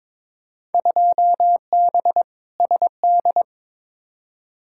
Morse code, heard as a keyed tone, about 22 wpm, 700 Hz. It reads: 2B SD
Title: Field Day Exchanges Encore - 22wpm